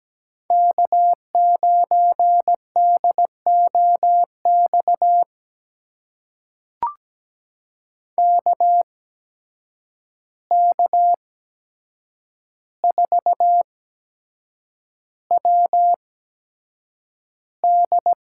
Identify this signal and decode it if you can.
Morse code, heard as a keyed tone, about 17 wpm, spaced as for 7 wpm, 700 Hz. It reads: K9DOX KK4WD
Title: USA Call Signs - 5 Characters - 8x Character Spacing - 17wpm